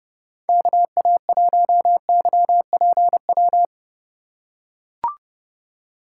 Morse code, heard as a keyed tone, about 30 wpm, 700 Hz. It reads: KA1YPW